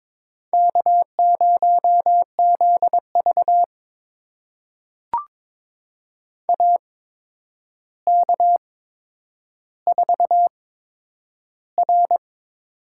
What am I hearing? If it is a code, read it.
Morse code, heard as a keyed tone, about 22 wpm, spaced as for 9 wpm, 700 Hz. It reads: K0ZV AK4R
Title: USA Call Signs - 4 Characters - 8x Character Spacing - 22wpm